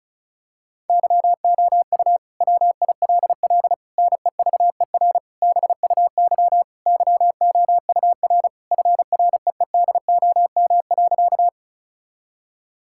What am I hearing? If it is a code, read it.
Morse code, heard as a keyed tone, about 36 wpm, 700 Hz. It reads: YOU WILL NEVER BUY YOUR FREEDOM.
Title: Sentences from Top 1000 Words - Review Part 5 - 35wpm